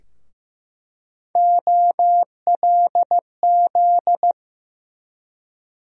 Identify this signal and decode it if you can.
Morse code, heard as a keyed tone, about 15 wpm, 700 Hz. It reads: OLZ
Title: USA Call Sign Postfixes - 3 Letters 15wpm